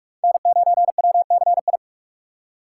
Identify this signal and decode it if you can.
Morse code, heard as a keyed tone, about 42 wpm, 700 Hz. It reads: N9WCI
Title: USA Call Signs - 2x Character Spacing - 45wpm